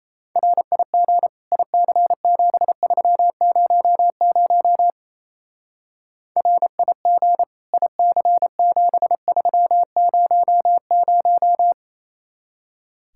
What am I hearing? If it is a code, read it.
Morse code, heard as a keyed tone, about 36 wpm, 700 Hz. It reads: RIG IC7300 RIG IC7300